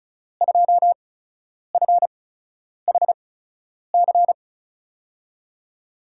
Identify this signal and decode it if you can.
Morse code, heard as a keyed tone, about 36 wpm, spaced as for 10 wpm, 700 Hz. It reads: 2FHC
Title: USA Call Signs - 5 Characters - 8x Character Spacing - 35wpm